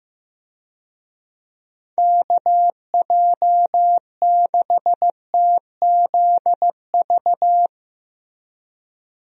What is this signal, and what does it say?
Morse code, heard as a keyed tone, about 15 wpm, 700 Hz. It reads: KJ6TZV
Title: USA Call Signs - 6 Characters - Repeated 2x - 15wpm